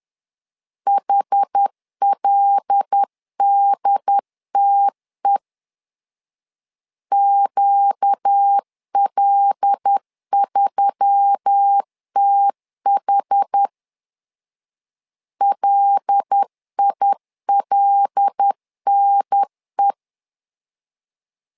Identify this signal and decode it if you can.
Morse code, heard as a keyed tone, about 10 wpm, 800 Hz. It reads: HLDTE QL3TH LILNE
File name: LEZIONI / 5 LBVC / 5 L 10wpm 2259